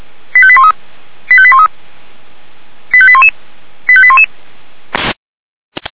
Five-tone selective call (ZVEI), heard as quick runs of five tones, tones of about 70 ms. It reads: 85813 85813 85811 85811